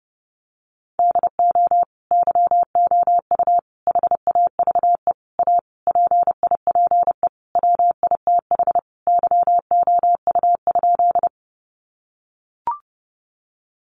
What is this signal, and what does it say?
Morse code, heard as a keyed tone, about 30 wpm, 700 Hz. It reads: DO YOU HAVE A PIPE WITH YOU?
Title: Sentences from Top 1500 Words 30wpm